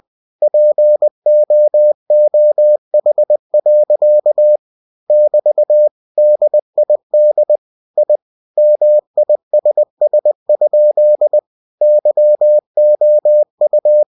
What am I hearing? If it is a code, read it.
Morse code, heard as a keyed tone, about 20 wpm, 600 Hz. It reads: POOH. <BT> DID I MISS? YOU